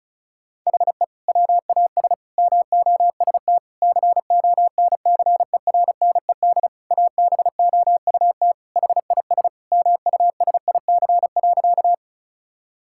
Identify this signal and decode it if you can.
Morse code, heard as a keyed tone, about 34 wpm, 690 Hz. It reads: HE WAS MOST CONCERNED ABOUT HIS MUSIC.
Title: Sentences from Top 1000 Words - Review Part 6 - 35wpm